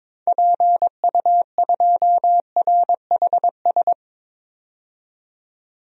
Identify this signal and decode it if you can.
Morse code, heard as a keyed tone, about 22 wpm, 700 Hz. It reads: PU2RHS
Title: DXCC-100 Call Signs - 6 Characters - 22wpm